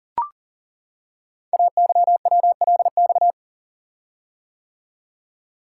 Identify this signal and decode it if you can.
Morse code, heard as a keyed tone, about 40 wpm, 700 Hz. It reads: AYWLX
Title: Code Groups - 5 Letters - 40wpm